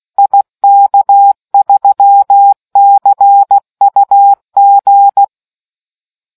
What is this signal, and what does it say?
Morse code, heard as a keyed tone, about 16 wpm, 800 Hz. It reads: IK3CUG